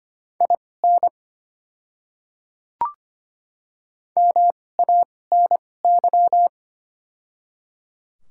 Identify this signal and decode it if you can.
Morse code, heard as a keyed tone, about 25 wpm, spaced as for 18 wpm, 690 Hz. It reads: IN MANY
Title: 2x Character Spacing - Top 200 Words - 25wpm